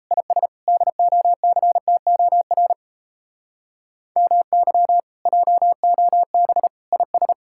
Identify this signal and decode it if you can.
Morse code, heard as a keyed tone, about 37 wpm, 700 Hz. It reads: IS DOCTOR MY JOB IS